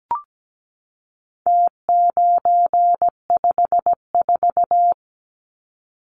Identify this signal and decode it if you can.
Morse code, heard as a keyed tone, about 17 wpm, 700 Hz. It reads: T954
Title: Numbers - Contest Style - 4-Digit 17wpm